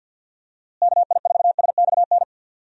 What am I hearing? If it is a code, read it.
Morse code, heard as a keyed tone, about 51 wpm, 700 Hz. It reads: KI4SXN
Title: USA Call Signs - 6 Characters - Repeated 3x - 50wpm